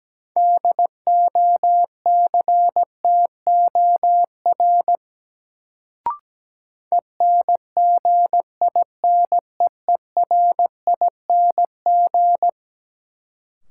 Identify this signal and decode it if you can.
Morse code, heard as a keyed tone, about 17 wpm, 700 Hz. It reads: DOCTOR ENGINEERING
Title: Sentences from Top 1500 Words - New Word Drill - 17wpm